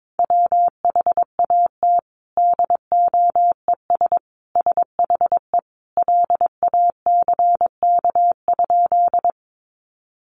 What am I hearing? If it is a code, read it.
Morse code, heard as a keyed tone, about 22 wpm, 700 Hz. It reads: WHAT DOES SHE LACK?
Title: Sentences from Top 1100 Words 22wpm